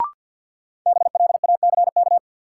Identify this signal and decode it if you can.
Morse code, heard as a keyed tone, about 53 wpm, 700 Hz. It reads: BLACK
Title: Sets of 5 Words Encore - New Word Drill - 50wpm